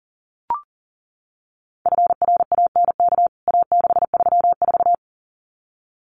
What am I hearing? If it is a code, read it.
Morse code, heard as a keyed tone, about 40 wpm, 700 Hz. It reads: FRANK A634
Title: CWT Exchange 40wpm